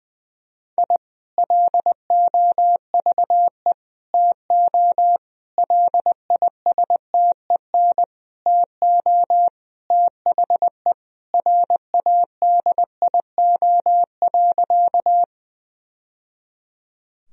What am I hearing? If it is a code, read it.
Morse code, heard as a keyed tone, about 20 wpm, 700 Hz. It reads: I LOVE TO LISTEN TO THE RADIO.
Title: Sentences from Top 1000 Words - Review Part 3 - 20wpm